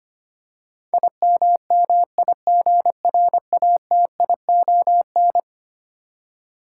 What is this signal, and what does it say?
Morse code, heard as a keyed tone, about 25 wpm, 700 Hz. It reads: IMMIGRATION
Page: morse-code-ninja-practice-25wpm